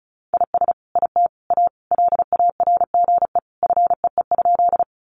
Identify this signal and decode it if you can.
Morse code, heard as a keyed tone, about 34 wpm, 700 Hz. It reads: IS IT A LARGE FEE?
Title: Sentences from Top 1500 Words 35wpm